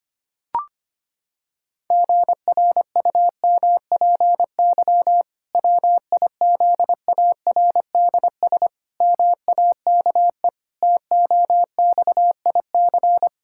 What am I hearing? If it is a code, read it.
Morse code, heard as a keyed tone, about 25 wpm, 700 Hz. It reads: GRUMPY WIZARDS MAKE TOXIC
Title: Pangram Challenge v2 - 25wpm